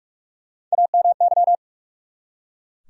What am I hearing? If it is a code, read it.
Morse code, heard as a keyed tone, about 43 wpm, 700 Hz. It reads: AMY
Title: USA Common Names - Max 3 Letters - 45wpm